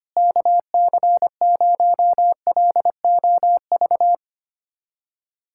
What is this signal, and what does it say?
Morse code, heard as a keyed tone, about 25 wpm, 700 Hz. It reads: KC0LOV
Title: SOTA QSO Practice 25wpm